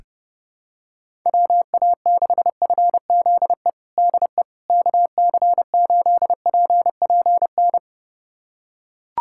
Morse code, heard as a keyed tone, about 32 wpm, 700 Hz. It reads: WA6FZE DE KC8PPN
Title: USA Callsign DE Callsign - 6 Characters - 30wpm